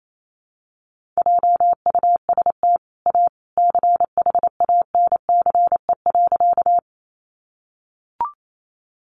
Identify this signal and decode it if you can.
Morse code, heard as a keyed tone, about 28 wpm, 700 Hz. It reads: JUST A CHANCE.
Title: Sentences from Top 800 Words 28wpm